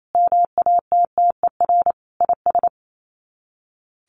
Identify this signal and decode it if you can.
Morse code, heard as a keyed tone, about 28 wpm, 700 Hz. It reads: MATTER IS
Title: Sets of 5 Words 28wpm